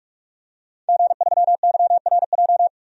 Morse code, heard as a keyed tone, about 46 wpm, 700 Hz. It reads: G3YRJ
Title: DXCC-100 Call Signs - 5 Characters - 45wpm